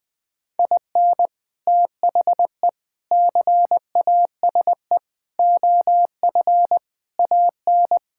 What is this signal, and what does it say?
Morse code, heard as a keyed tone, about 20 wpm, 700 Hz. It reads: IN THE CASE OF AN
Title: Sentences with 3-Word Phrases - 20wpm